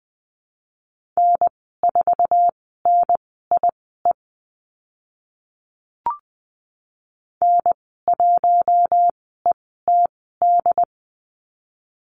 Morse code, heard as a keyed tone, about 20 wpm, spaced as for 15 wpm, 700 Hz. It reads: N4NIE N1ETD
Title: 069 Lesson 12g - Letter d - Call Signs 20wpm